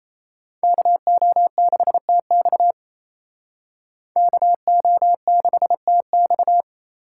Morse code, heard as a keyed tone, about 36 wpm, 700 Hz. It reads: KO6TX KO6TX